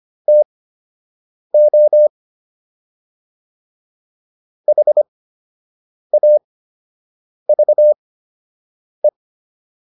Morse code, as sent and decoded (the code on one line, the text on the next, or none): - --- / .... .- ...- .
TO HAVE